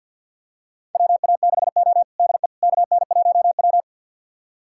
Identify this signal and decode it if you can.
Morse code, heard as a keyed tone, about 52 wpm, 700 Hz. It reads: WA6O DE KN1W
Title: USA Callsign DE Callsign - 4 Characters - 50wpm